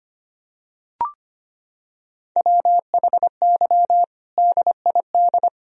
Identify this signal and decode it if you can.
Morse code, heard as a keyed tone, about 25 wpm, 700 Hz. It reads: WHY DID
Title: Sentences from Top 1000 Words - Review Part 4 - 25wpm